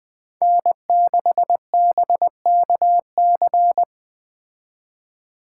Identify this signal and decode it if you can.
Morse code, heard as a keyed tone, about 20 wpm, 700 Hz. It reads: N6BKC
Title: Speed Racing - USA Call Signs 17wpm